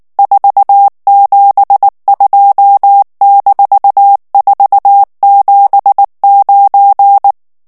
Morse code, heard as a keyed tone, about 19 wpm, 800 Hz. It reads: 472-479
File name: ANGLAIS texte clair 3 vitesse 18 mots minute